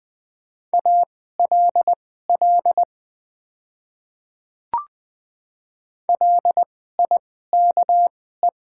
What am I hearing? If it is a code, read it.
Morse code, heard as a keyed tone, about 20 wpm, spaced as for 14 wpm, 700 Hz. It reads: ALL LIKE
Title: 2x Character Spacing - Top 100 Words - 20wpm